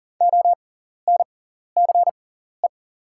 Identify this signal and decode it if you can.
Morse code, heard as a keyed tone, about 38 wpm, spaced as for 14 wpm, 700 Hz. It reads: ONCE
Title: 6x Character Spacing - Top 500 Words - 40wpm